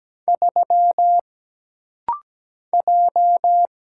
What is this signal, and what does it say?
Morse code, heard as a keyed tone, about 17 wpm, 700 Hz. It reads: 3 J
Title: Single Letter-Number Alt - Rapid-Fire 17wpm